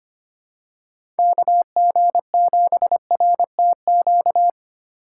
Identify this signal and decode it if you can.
Morse code, heard as a keyed tone, about 25 wpm, 700 Hz. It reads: KG7RTQ
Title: USA Call Signs - 6 Characters - 3x Character Spacing - 25wpm